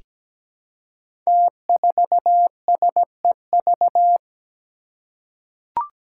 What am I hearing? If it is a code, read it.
Morse code, heard as a keyed tone, about 17 wpm, 700 Hz. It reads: T4SEV
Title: Five Challenging Letter-Number Combinations 17wpm